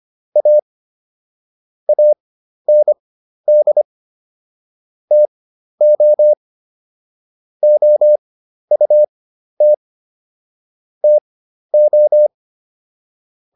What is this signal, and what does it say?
Morse code, heard as a keyed tone, about 25 wpm, spaced as for 12 wpm, 600 Hz. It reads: A AND TO OUT TO